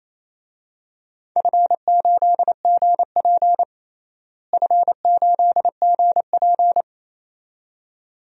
Morse code, heard as a keyed tone, about 28 wpm, 700 Hz. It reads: F8GP F8GP
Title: DXCC-100 Call Signs - 4 Characters - Repeated 2x - 28wpm